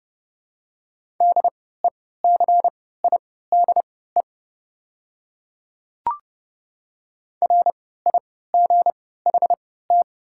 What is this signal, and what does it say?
Morse code, heard as a keyed tone, about 32 wpm, spaced as for 17 wpm, 700 Hz. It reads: DECIDE RIGHT
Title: 3x Character Spacing - Top 500 Words - 30wpm